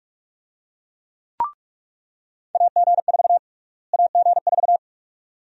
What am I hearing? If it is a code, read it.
Morse code, heard as a keyed tone, about 47 wpm, 700 Hz. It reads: AG4 AG4
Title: USA Call Sign Prefixes - Repeated 2x - 45wpm